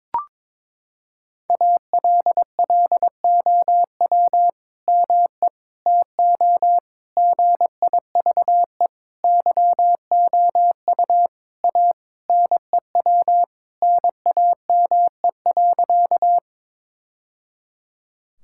Morse code, heard as a keyed tone, about 22 wpm, 700 Hz. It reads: ALLOW ME TO GIVE YOU A NEW NAME.
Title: Sentences from Top 1000 Words - Review Part 5 - 22wpm